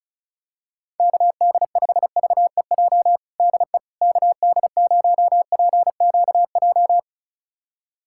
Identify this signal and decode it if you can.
Morse code, heard as a keyed tone, about 36 wpm, 700 Hz. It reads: KD5VEJ DE KD0PQJ
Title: USA Callsign DE Callsign - 6 Characters - 35wpm